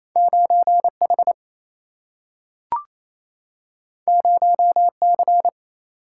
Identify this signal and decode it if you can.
Morse code, heard as a keyed tone, about 28 wpm, 700 Hz. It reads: 9H 0C
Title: Single-Letter Single-Number 28wpm